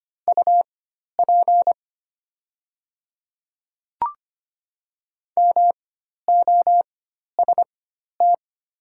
Morse code, heard as a keyed tone, about 25 wpm, spaced as for 12 wpm, 700 Hz. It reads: UP MOST
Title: 4x Character Spacing - Top 100 Words - 25wpm